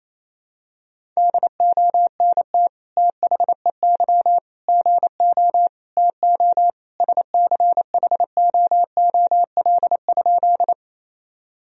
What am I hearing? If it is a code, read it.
Morse code, heard as a keyed tone, about 28 wpm, 700 Hz. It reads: DONT THEY GO TO SCHOOL?
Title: Sentences from Top 1000 Words - Review Part 1 - 28wpm